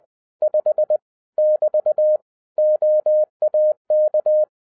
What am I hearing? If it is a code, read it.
Morse code, heard as a keyed tone, about 20 wpm, 610 Hz. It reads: 5 <BT> OAK
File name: 2011-07-mp3 / n23 QRP OTAKU 20wpm